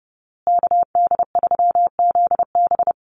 Morse code, heard as a keyed tone, about 30 wpm, 700 Hz. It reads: KD3ZB
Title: USA Call Signs Repeated 2x 30wpm